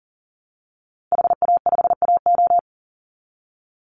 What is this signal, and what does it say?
Morse code, heard as a keyed tone, about 40 wpm, 700 Hz. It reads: HA5AO
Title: Call Signs - 40wpm